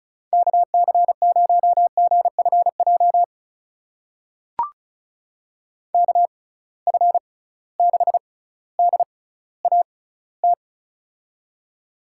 Morse code, heard as a keyed tone, about 36 wpm, spaced as for 18 wpm, 700 Hz. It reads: KC0GFJ KF6DAT
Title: USA Call Signs - 6 Characters - 6x Character Spacing - 35wpm